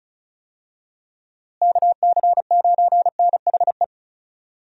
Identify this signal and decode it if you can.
Morse code, heard as a keyed tone, about 36 wpm, 700 Hz. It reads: KC9NHE
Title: USA Call Signs - 6 Characters - Repeated 3x - 35wpm